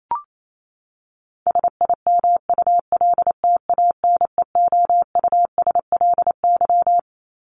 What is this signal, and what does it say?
Morse code, heard as a keyed tone, about 28 wpm, 700 Hz. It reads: SIMULTANEOUSLY
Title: LY Suffix - Top 100 Words with Base Words 28wpm